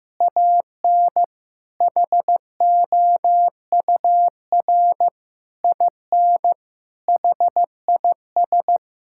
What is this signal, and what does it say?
Morse code, heard as a keyed tone, about 16 wpm, 700 Hz. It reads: AN HOUR IN HIS
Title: Sentences from Top 1000 Words - Review Part 6 - 15wpm